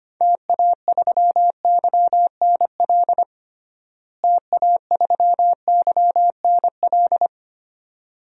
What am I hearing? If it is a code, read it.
Morse code, heard as a keyed tone, about 25 wpm, 700 Hz. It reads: TA3YNL TA3YNL
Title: DXCC-100 Call Signs - 6 Characters - Repeated 2x - 25wpm